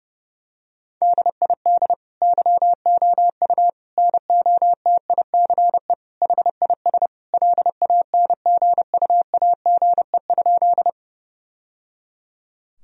Morse code, heard as a keyed tone, about 30 wpm, 700 Hz. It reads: DID YOU NOTICE HIS LANGUAGE?
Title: Sentences from Top 1000 Words - Review Part 3 - 30wpm